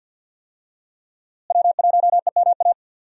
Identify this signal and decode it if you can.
Morse code, heard as a keyed tone, about 52 wpm, 700 Hz. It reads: W1EMA
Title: USA Call Signs - 5 Characters - Repeated 3x - 50wpm